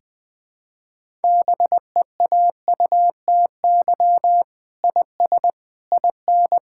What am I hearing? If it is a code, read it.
Morse code, heard as a keyed tone, about 20 wpm, 700 Hz. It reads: BEAUTY IS IN
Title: Sets of 8 Words - 20wpm